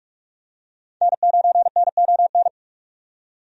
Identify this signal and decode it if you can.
Morse code, heard as a keyed tone, about 46 wpm, 700 Hz. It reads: N9NON
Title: USA Call Signs - 5 Characters - 45wpm